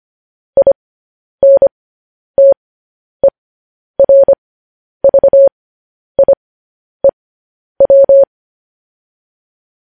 Morse code, heard as a keyed tone, about 25 wpm, spaced as for 10 wpm, 550 Hz. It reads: INTERVIEW
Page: news-headlines-in-morse-code-at-10-wpm